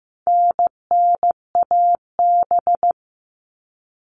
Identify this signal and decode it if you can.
Morse code, heard as a keyed tone, about 15 wpm, 700 Hz. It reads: NNAB
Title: All Cut Numbers - 4-Digit 15wpm